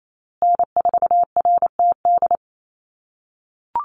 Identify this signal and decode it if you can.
Morse code, heard as a keyed tone, about 28 wpm, 700 Hz. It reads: N4RTD
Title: USA Call Signs Repeated 3x 28wpm